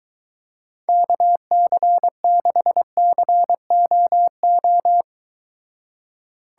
Morse code, heard as a keyed tone, about 23 wpm, 700 Hz. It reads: KC6COO